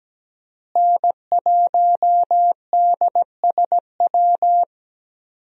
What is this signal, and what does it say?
Morse code, heard as a keyed tone, about 17 wpm, 700 Hz. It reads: N1DSW
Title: USA Call Signs - 5 Characters - Repeated 3x - 17wpm